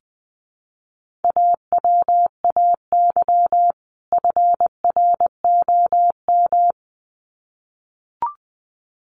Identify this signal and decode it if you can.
Morse code, heard as a keyed tone, about 20 wpm, 700 Hz. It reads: AWAY FROM
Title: Sets of 2 Words 20wpm